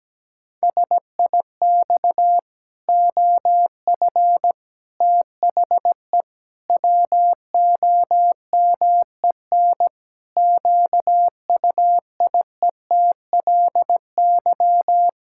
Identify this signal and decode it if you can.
Morse code, heard as a keyed tone, about 17 wpm, 700 Hz. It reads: SIX OF THE WOMEN QUIETLY
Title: Pangram Challenge v2 - 17wpm